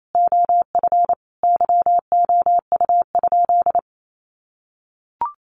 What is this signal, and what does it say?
Morse code, heard as a keyed tone, about 28 wpm, 700 Hz. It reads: OF YOU?
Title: Sentences from Top 300 Words 28wpm